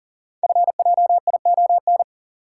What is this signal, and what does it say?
Morse code, heard as a keyed tone, about 40 wpm, 700 Hz. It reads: FJION
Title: Code Groups - 5 Letters - 40wpm